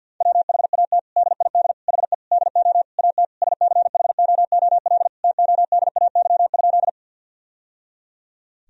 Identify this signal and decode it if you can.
Morse code, heard as a keyed tone, about 52 wpm, 700 Hz. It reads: WHAT DID HE DO AT SCHOOL TODAY?
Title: Sentences from Top 1000 Words - Review Part 1 - 50wpm